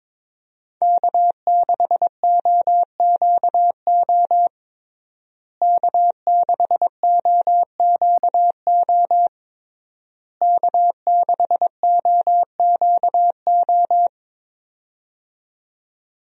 Morse code, heard as a keyed tone, about 22 wpm, 700 Hz. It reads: K6OQO K6OQO K6OQO
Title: USA Call Signs - 5 Characters - Repeated 3x - 22wpm